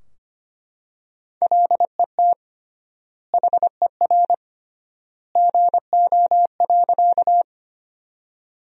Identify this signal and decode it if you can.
Morse code, heard as a keyed tone, about 25 wpm, 700 Hz. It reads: LET HER GO.
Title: 3x Word-Spacing - Sentences from Top 300 Words - 25wpm